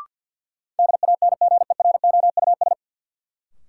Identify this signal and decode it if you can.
Morse code, heard as a keyed tone, about 52 wpm, 700 Hz. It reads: DANGEROUS
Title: Sentences from Top 1200 Words - New Word Drill - 50wpm